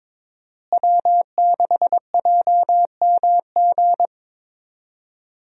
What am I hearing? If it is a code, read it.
Morse code, heard as a keyed tone, about 22 wpm, 700 Hz. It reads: W6JMG
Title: USA Call Signs - 5 Characters - Repeated 2x - 22wpm